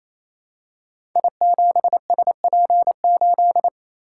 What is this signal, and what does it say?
Morse code, heard as a keyed tone, about 27 wpm, 700 Hz. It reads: I7SP8
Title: Code Groups - 5 Letter:Numbers - 28wpm